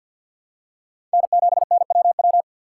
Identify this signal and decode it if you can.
Morse code, heard as a keyed tone, about 52 wpm, 700 Hz. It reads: N7NWW
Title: USA Call Signs - 5 Characters - 2x Character Spacing - 50wpm